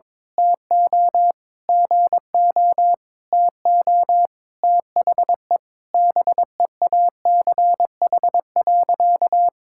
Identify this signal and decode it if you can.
Morse code, heard as a keyed tone, about 22 wpm, 700 Hz. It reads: TO GO TO THE BEACH.
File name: practice / 22wpm / Sentences with 4-Word Phrases - 22wpm